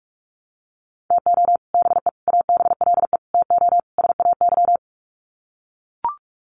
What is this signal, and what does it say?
Morse code, heard as a keyed tone, about 44 wpm, 700 Hz. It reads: TO BE ABLE TO SAY
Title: Sets of 5 Words Encore2 45wpm